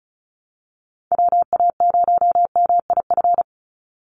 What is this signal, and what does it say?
Morse code, heard as a keyed tone, about 34 wpm, 700 Hz. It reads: WA0MIF
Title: USA Call Signs 35wpm